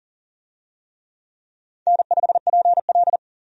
Morse code, heard as a keyed tone, about 40 wpm, 700 Hz. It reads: N5PL
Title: USA Call Signs - 4 Characters - 40wpm